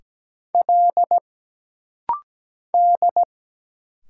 Morse code, heard as a keyed tone, about 17 wpm, 700 Hz. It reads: L D
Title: Single Letter-Number Alt - Rapid-Fire 17wpm